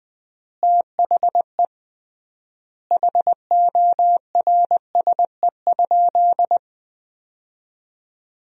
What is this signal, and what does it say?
Morse code, heard as a keyed tone, about 21 wpm, 700 Hz. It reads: THE HORSE?
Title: 3x Word-Spacing - Sentences from Top 500 Words - 20wpm